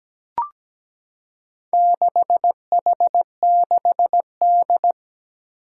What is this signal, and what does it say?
Morse code, heard as a keyed tone, about 17 wpm, 700 Hz. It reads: 6H6D
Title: Four Challenging Letter-Number Combinations 17wpm